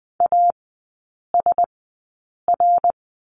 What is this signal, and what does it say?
Morse code, heard as a keyed tone, about 20 wpm, spaced as for 8 wpm, 700 Hz. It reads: ASR
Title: 053 Lesson 10e - Letter r - Three Characters 20wpm